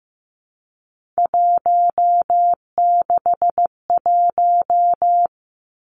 Morse code, heard as a keyed tone, about 16 wpm, 700 Hz. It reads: 161
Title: Three Challenging Letter-Number Combinations 15wpm